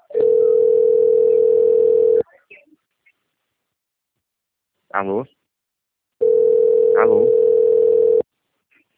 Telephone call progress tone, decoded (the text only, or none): ringback tone